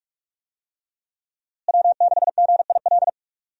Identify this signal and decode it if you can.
Morse code, heard as a keyed tone, about 42 wpm, 700 Hz. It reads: W6GIL
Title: USA Call Signs - 5 Characters - Repeated 2x - 45wpm